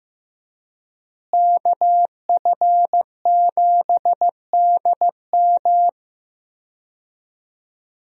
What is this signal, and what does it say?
Morse code, heard as a keyed tone, about 15 wpm, 700 Hz. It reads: KF7DM
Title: USA Call Signs - 5 Characters - Repeated 3x - 15wpm